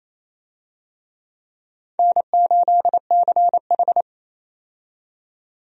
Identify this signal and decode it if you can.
Morse code, heard as a keyed tone, about 28 wpm, 700 Hz. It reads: N8CH
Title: USA Call Signs - 4 Characters - 28wpm